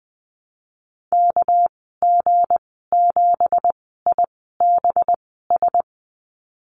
Morse code, heard as a keyed tone, about 20 wpm, spaced as for 15 wpm, 700 Hz. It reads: KG7IBS
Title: 200 Lesson 32g - Letter k - Call Signs 20wpm